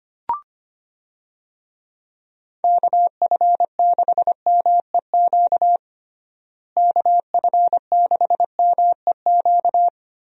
Morse code, heard as a keyed tone, about 25 wpm, 700 Hz. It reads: KF6MEQ KF6MEQ
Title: USA Call Signs - 6 Characters - Repeated 2x - 25wpm